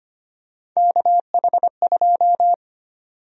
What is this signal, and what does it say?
Morse code, heard as a keyed tone, about 25 wpm, 700 Hz. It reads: KH2
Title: USA Call Sign Prefixes - 25wpm